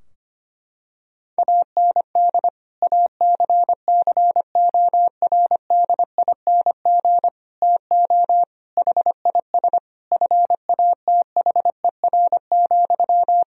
Morse code, heard as a keyed tone, about 25 wpm, 700 Hz. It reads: AND ACCORDING TO HIS FATHER,
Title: Sentences from Top 1000 Words - Review Part 4 - 25wpm